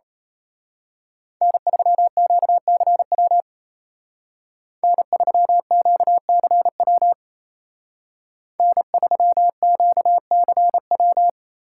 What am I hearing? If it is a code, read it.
Morse code, heard as a keyed tone, about 32 wpm, 700 Hz. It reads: N3QCW N3QCW N3QCW
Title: Speed Racing - USA 1x3 Call Signs 28wpm